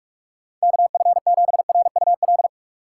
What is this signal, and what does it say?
Morse code, heard as a keyed tone, about 46 wpm, 700 Hz. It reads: KF7RUL
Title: USA Call Signs - 6 Characters - 3x Character Spacing - 45wpm